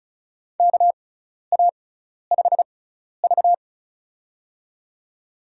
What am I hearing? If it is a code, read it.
Morse code, heard as a keyed tone, about 36 wpm, spaced as for 12 wpm, 700 Hz. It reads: KA5V
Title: USA Call Signs - 4 Characters - 6x Character Spacing - 35wpm